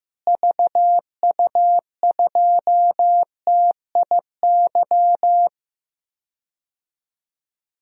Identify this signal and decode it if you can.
Morse code, heard as a keyed tone, about 15 wpm, 700 Hz. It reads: VU2TIY
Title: DXCC-100 Call Signs - 6 Characters - Repeated 3x - 15wpm